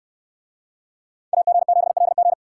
Morse code, heard as a keyed tone, about 69 wpm, 700 Hz. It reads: AG7LG